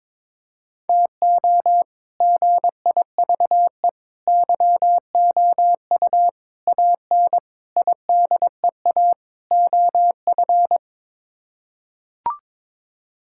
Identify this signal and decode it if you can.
Morse code, heard as a keyed tone, about 22 wpm, 700 Hz. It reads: TO GIVE YOU AN IDEA OF
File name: practice / 22wpm / Sets of 6 Words - 22wpm